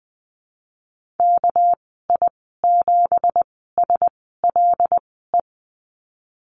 Morse code, heard as a keyed tone, about 20 wpm, spaced as for 15 wpm, 700 Hz. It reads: KI7SLE
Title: 213 Lesson 34g - Number 8 - Call Signs 20wpm